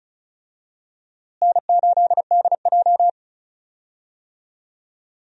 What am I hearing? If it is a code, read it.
Morse code, heard as a keyed tone, about 36 wpm, 700 Hz. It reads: N8DJ
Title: USA Call Signs - 4 Characters - Repeated 2x - 35wpm